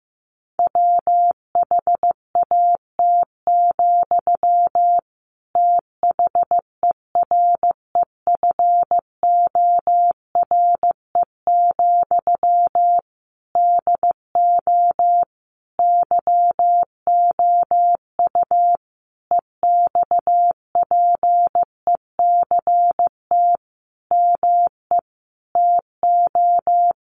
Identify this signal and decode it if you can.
Morse code, heard as a keyed tone, about 15 wpm, 700 Hz. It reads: WHAT, THEREFORE, DO YOU EXPECT ME TO
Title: Sentences from Top 1100 Words 15wpm